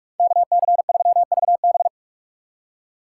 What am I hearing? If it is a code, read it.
Morse code, heard as a keyed tone, about 42 wpm, 700 Hz. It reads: KC3VB